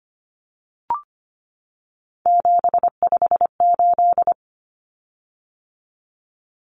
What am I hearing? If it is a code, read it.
Morse code, heard as a keyed tone, about 25 wpm, 700 Hz. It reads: 758
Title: Common Cut Numbers - 3-Digit 25wpm